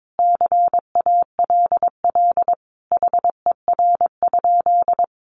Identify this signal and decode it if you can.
Morse code, heard as a keyed tone, about 22 wpm, 700 Hz. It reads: CALL HER?
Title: Sentences from Top 100 Words 22wpm